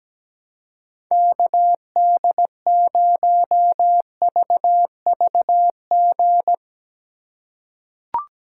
Morse code, heard as a keyed tone, about 17 wpm, 700 Hz. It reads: KD0VVG
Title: USA Call Signs - 6 Characters - 4x Character Spacing - 17wpm